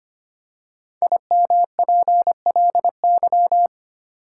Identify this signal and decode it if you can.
Morse code, heard as a keyed tone, about 25 wpm, 700 Hz. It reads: IMPLY
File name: 5 Letter Words - First 3 Letters Given - 25wpm